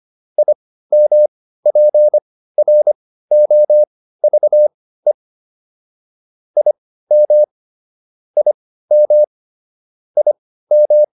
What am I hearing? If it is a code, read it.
Morse code, heard as a keyed tone, about 25 wpm, spaced as for 15 wpm, 600 Hz. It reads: IMPROVE IM IM IM